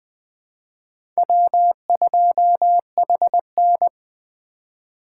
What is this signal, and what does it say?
Morse code, heard as a keyed tone, about 21 wpm, 700 Hz. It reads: W2HN